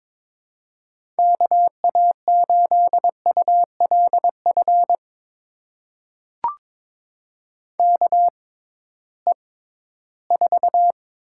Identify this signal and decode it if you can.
Morse code, heard as a keyed tone, about 22 wpm, spaced as for 14 wpm, 700 Hz. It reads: KA8ULF KE4
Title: USA Call Signs - 6 Characters - 6x Character Spacing - 22wpm